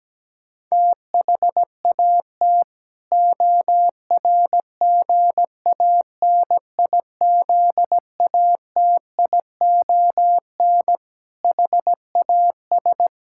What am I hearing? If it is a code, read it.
Morse code, heard as a keyed tone, about 17 wpm, 700 Hz. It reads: THAT ORGANIZATION HAS
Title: Sentences from Top 1000 Words - Review Part 6 - 17wpm